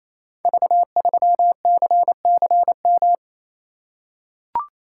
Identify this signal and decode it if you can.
Morse code, heard as a keyed tone, about 27 wpm, 700 Hz. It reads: V3CCM
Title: Code Groups - 5 Letter:Numbers - 28wpm